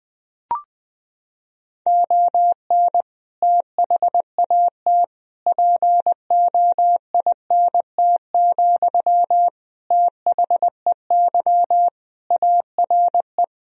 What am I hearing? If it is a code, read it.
Morse code, heard as a keyed tone, about 20 wpm, 700 Hz. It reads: ON THAT POINT, THEY ARE
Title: Sentences from Top 1000 Words - Review Part 4 - 20wpm